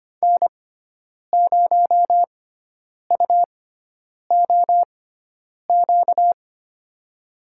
Morse code, heard as a keyed tone, about 25 wpm, spaced as for 9 wpm, 700 Hz. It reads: N0UOQ
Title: USA Call Signs - 5 Characters - 6x Character Spacing - 25wpm